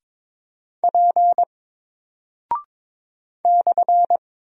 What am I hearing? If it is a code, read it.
Morse code, heard as a keyed tone, about 22 wpm, 700 Hz. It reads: P /
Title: Single Character (Letter:Number:Punctuation:Prosign) Alt 22wpm